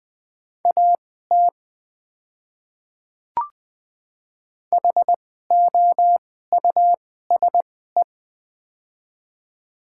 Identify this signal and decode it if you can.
Morse code, heard as a keyed tone, about 20 wpm, spaced as for 14 wpm, 700 Hz. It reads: AT HOUSE
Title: 2x Character Spacing - Top 200 Words - 20wpm